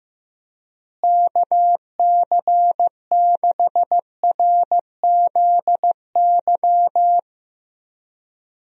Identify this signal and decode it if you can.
Morse code, heard as a keyed tone, about 15 wpm, 700 Hz. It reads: KC6RZY
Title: USA Call Signs - 6 Characters - Repeated 3x - 15wpm